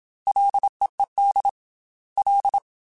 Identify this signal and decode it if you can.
Morse code, heard as a keyed tone, about 27 wpm, 800 Hz. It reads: LEED L
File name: Ls / Single L m3s / Single L @25